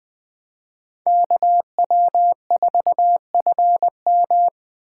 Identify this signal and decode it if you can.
Morse code, heard as a keyed tone, about 21 wpm, 700 Hz. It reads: KW4FM
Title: 2023 POTA Hunter Call Signs - 20wpm